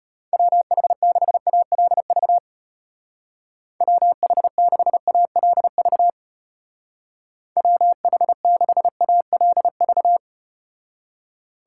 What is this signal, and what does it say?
Morse code, heard as a keyed tone, about 33 wpm, 700 Hz. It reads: WH6ALV WH6ALV WH6ALV